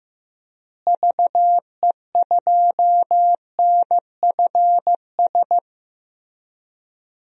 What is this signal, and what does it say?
Morse code, heard as a keyed tone, about 15 wpm, 700 Hz. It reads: VE2NFS
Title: DXCC-100 Call Signs - 6 Characters - Repeated 2x - 15wpm